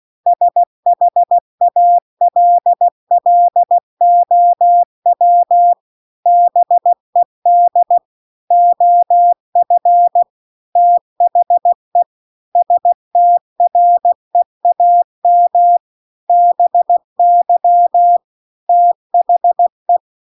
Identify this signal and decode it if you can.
Morse code, heard as a keyed tone, about 16 wpm, 700 Hz. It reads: SHALLOW BED OF THE STREAM BY THE